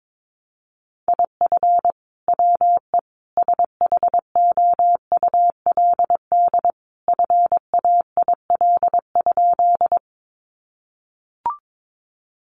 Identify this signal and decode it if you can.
Morse code, heard as a keyed tone, about 22 wpm, 700 Hz. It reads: IF WE SHOULD FAIL?
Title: Sentences from Top 1100 Words 22wpm